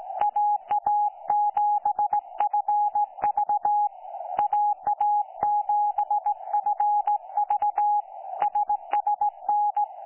Morse code, heard as a keyed tone, about 17 wpm, 830 Hz. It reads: AA7FV AA7FV SSN
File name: 09 38Z 18088.0kHz k3fef darrel